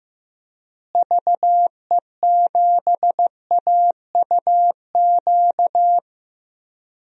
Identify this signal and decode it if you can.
Morse code, heard as a keyed tone, about 15 wpm, 700 Hz. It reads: VE7AUQ
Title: DXCC-100 Call Signs - 6 Characters - Repeated 3x - 15wpm